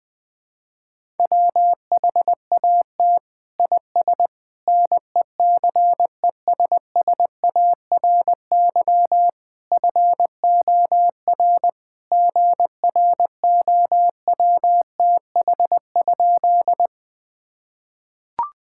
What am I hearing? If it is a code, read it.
Morse code, heard as a keyed tone, about 20 wpm, 700 Hz. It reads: WHAT IS NECESSARY FOR GROWTH?
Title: Sentences from Top 1000 Words - Review Part 5 - 20wpm